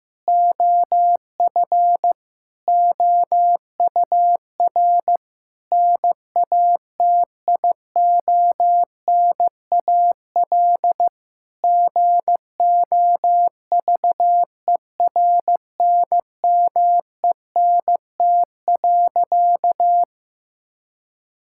Morse code, heard as a keyed tone, about 15 wpm, 700 Hz. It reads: OF OUR NATIONAL GOVERNMENT.
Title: Sentences from Top 1000 Words - Review Part 5 - 15wpm